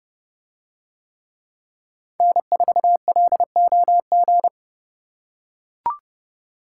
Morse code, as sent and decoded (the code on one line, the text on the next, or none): -. ....- .-.. --- --.
N4LOG